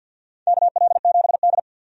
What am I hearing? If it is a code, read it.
Morse code, heard as a keyed tone, about 52 wpm, 700 Hz. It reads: KL7D
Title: DXCC-100 Call Signs - 4 Characters - 50wpm